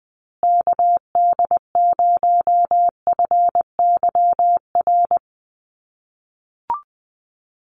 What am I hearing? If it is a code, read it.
Morse code, heard as a keyed tone, about 20 wpm, 700 Hz. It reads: KD0FYR
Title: USA Call Signs Repeated 2x 20wpm